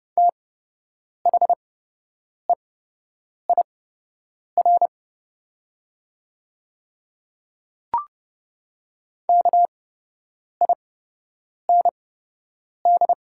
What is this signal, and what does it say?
Morse code, heard as a keyed tone, about 30 wpm, spaced as for 8 wpm, 700 Hz. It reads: THEIR KIND
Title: 8x Character Spacing - Top 200 Words - 30wpm